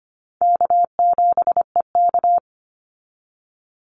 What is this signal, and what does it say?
Morse code, heard as a keyed tone, about 25 wpm, 700 Hz. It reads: K7EK
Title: Easy Call Signs - 25wpm